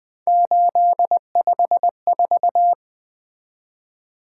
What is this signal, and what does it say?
Morse code, heard as a keyed tone, about 24 wpm, 700 Hz. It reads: 854
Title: Speed-Racing - Three Letter-Number Combinations - 20wpm